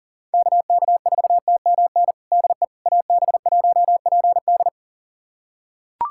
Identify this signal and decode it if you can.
Morse code, heard as a keyed tone, about 40 wpm, 700 Hz. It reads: KK4TMN DE AB1PD
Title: USA Callsign DE Callsign 40wpm